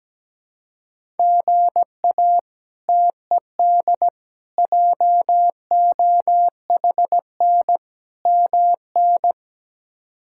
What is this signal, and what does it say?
Morse code, heard as a keyed tone, about 17 wpm, 700 Hz. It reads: GA TED JOHN MN
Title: K1USN SST QSO Practice - 17wpm